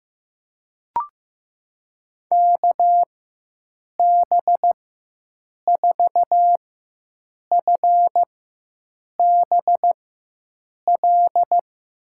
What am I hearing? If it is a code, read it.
Morse code, heard as a keyed tone, about 15 wpm, spaced as for 7 wpm, 700 Hz. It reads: KB4FBL
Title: USA Call Signs - 6 Characters - 4x Character Spacing - 15wpm